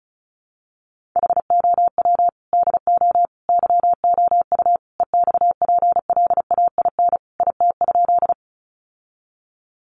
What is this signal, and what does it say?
Morse code, heard as a keyed tone, about 34 wpm, 700 Hz. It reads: HOW DO YOU EXPLAIN IT?
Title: Sentences from Top 1100 Words 35wpm